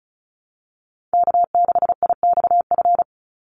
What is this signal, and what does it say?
Morse code, heard as a keyed tone, about 34 wpm, 700 Hz. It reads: K6IXF
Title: USA Call Signs 35wpm